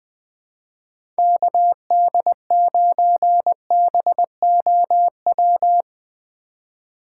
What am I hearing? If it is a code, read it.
Morse code, heard as a keyed tone, about 20 wpm, 700 Hz. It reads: KD9BOW
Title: USA Call Signs - 6 Characters - 4x Character Spacing - 20wpm